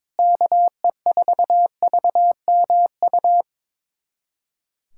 Morse code, heard as a keyed tone, about 22 wpm, 700 Hz. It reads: KE4VMU
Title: USA Callsign DE Callsign - 6 Characters - 22wpm